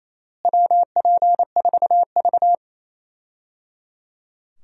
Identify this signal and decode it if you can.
Morse code, heard as a keyed tone, about 28 wpm, 700 Hz. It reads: WP4V
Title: USA Call Signs - 4 Characters - Repeated 3x - 28wpm